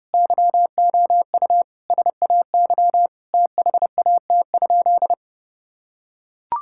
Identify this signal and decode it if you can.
Morse code, heard as a keyed tone, about 32 wpm, 700 Hz. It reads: YOU SAY THAT?
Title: Sentences from Top 1000 Words - Review Part 5 - 30wpm